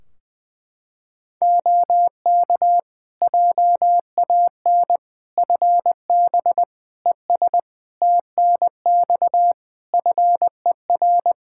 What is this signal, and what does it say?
Morse code, heard as a keyed tone, about 20 wpm, 700 Hz. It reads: OK JAN FB ES TNX FER
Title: K1USN SST QSO Practice 20wpm